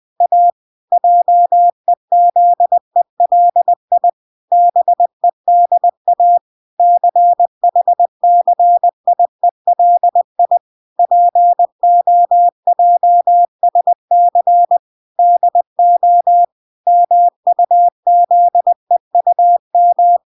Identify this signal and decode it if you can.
Morse code, heard as a keyed tone, about 20 wpm, 700 Hz. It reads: A JEZELI BEDA CHCIELI POJSC DO MUZEUM
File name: recordings / 20wpm0000